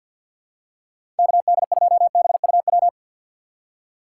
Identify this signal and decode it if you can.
Morse code, heard as a keyed tone, about 52 wpm, 700 Hz. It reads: KD2BUW
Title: USA Call Signs - 6 Characters - 3x Character Spacing - 50wpm